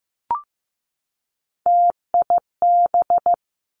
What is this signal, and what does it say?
Morse code, heard as a keyed tone, about 15 wpm, 700 Hz. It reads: TIB
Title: Three Challenging Letter-Number Combinations 15wpm